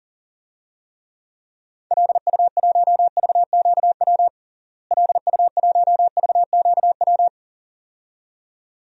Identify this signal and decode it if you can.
Morse code, heard as a keyed tone, about 37 wpm, 700 Hz. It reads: LU1VQW LU1VQW
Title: DXCC-100 Call Signs - 6 Characters - Repeated 2x - 40wpm